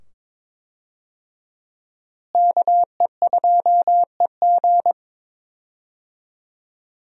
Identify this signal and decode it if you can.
Morse code, heard as a keyed tone, about 22 wpm, 700 Hz. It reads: KE2EG
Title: USA Call Signs - 5 Characters - Repeated 2x - 22wpm